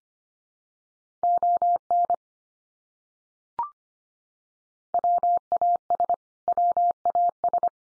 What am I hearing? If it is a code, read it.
Morse code, heard as a keyed tone, about 25 wpm, 700 Hz. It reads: ON WAS WAS
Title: Top 100 Words 25wpm x3